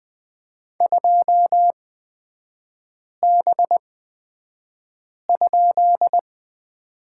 Morse code, heard as a keyed tone, about 20 wpm, spaced as for 5 wpm, 700 Hz. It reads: 2B?
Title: 255 Lesson 41e - Period - Three Characters 20wpm